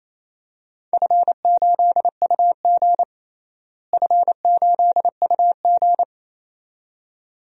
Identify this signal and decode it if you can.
Morse code, heard as a keyed tone, about 28 wpm, 700 Hz. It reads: F8UG F8UG
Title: DXCC-100 Call Signs - 4 Characters - Repeated 2x - 28wpm